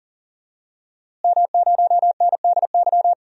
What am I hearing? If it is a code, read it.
Morse code, heard as a keyed tone, about 40 wpm, 700 Hz. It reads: M0NDY
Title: DXCC-100 Call Signs - 5 Characters - Repeated 3x - 40wpm